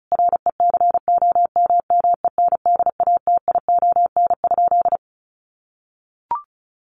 Morse code, heard as a keyed tone, about 34 wpm, 700 Hz. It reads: RECOMMENDATION?
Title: Sentences from Top 1400 Words 35wpm